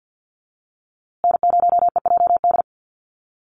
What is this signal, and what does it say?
Morse code, heard as a keyed tone, about 51 wpm, 700 Hz. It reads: N0EJD
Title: USA Call Signs Repeated 2x 50wpm